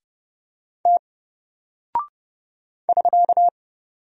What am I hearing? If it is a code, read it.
Morse code, heard as a keyed tone, about 30 wpm, 700 Hz. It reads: T <SK>
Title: Single Character (Letter:Number:Punctuation:Prosign) 30wpm